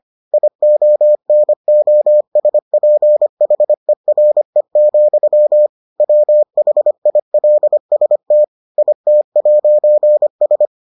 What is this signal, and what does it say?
Morse code, heard as a keyed tone, about 25 wpm, 600 Hz. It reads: IONOSPHERE, WHILST IT'S